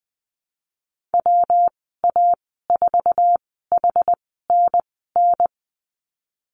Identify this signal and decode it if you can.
Morse code, heard as a keyed tone, about 20 wpm, spaced as for 15 wpm, 700 Hz. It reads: WA4HNN
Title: 147 Lesson 24g - Letter y - Call Signs 20wpm